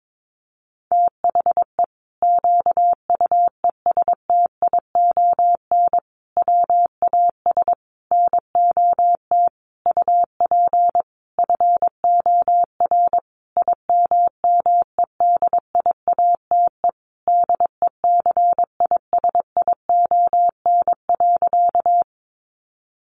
Morse code, heard as a keyed tone, about 22 wpm, 700 Hz. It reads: THE QUESTION WAS NOT UP FOR IMMEDIATE DECISION.
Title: Sentences from Top 1200 Words 22wpm